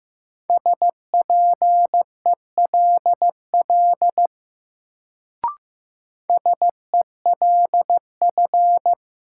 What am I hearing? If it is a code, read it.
Morse code, heard as a keyed tone, about 15 wpm, 700 Hz. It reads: SPELL SELF
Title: Sentences from Top 1000 Words - Review Part 1 - New Word Drill - 15wpm